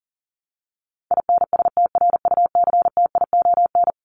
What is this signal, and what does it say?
Morse code, heard as a keyed tone, about 40 wpm, 700 Hz. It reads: INSTRUCTION
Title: Top 1751-2000 Words 40wpm